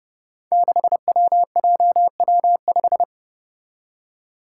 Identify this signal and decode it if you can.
Morse code, heard as a keyed tone, about 30 wpm, 700 Hz. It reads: 6WJW5
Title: Five Challenging Letter-Number Combinations 30wpm